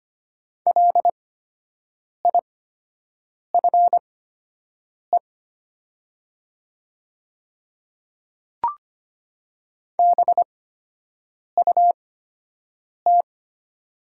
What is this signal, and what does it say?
Morse code, heard as a keyed tone, about 25 wpm, spaced as for 7 wpm, 700 Hz. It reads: LIFE BUT
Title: 8x Character Spacing - Top 300 Words - 25wpm